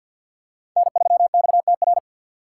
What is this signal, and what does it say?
Morse code, heard as a keyed tone, about 50 wpm, 700 Hz. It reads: N3XTR
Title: USA Call Signs - 5 Characters - 3x Character Spacing - 50wpm